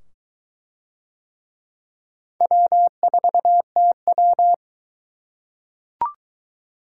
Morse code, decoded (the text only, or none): W4TW